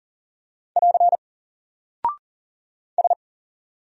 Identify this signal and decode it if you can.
Morse code, heard as a keyed tone, about 40 wpm, 700 Hz. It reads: <AR> S
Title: Single Character (Letter:Number:Punctuation:Prosign) 40wpm